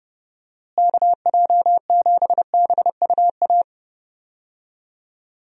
Morse code, heard as a keyed tone, about 30 wpm, 700 Hz. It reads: KJ7BUA
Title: USA Call Signs - 6 Characters - Repeated 3x - 30wpm